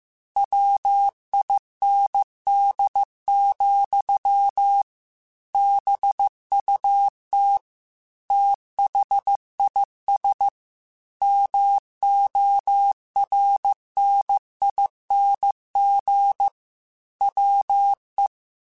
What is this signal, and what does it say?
Morse code, heard as a keyed tone, about 15 wpm, 800 Hz. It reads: WIND, BUT THIS MORNING WE
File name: Weather Sets / Weather Sets-mp3s / Weather 3 17 @13